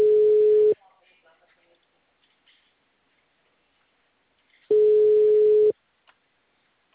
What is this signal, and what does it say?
Ringback tone (Europe, 425 Hz), with speech.